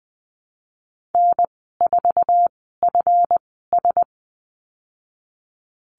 Morse code, heard as a keyed tone, about 20 wpm, spaced as for 15 wpm, 700 Hz. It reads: N4FS
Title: 161 Lesson 26g - Letter g - Call Signs 20wpm